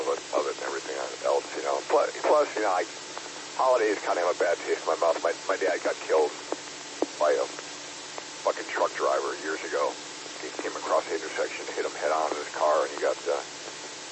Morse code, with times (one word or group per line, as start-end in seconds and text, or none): none